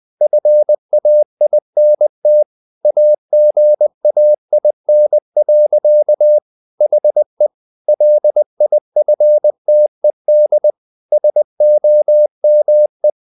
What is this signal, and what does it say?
Morse code, heard as a keyed tone, about 20 wpm, 600 Hz. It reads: FAINT AGAIN. HE LIFTED SOME
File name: the old man and the sea part 2 20wpm